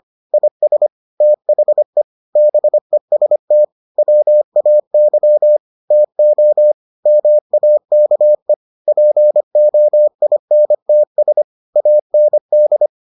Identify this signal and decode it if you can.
Morse code, heard as a keyed tone, about 25 wpm, 600 Hz. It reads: IS THE BEST WAY TO MAKE POINTS AND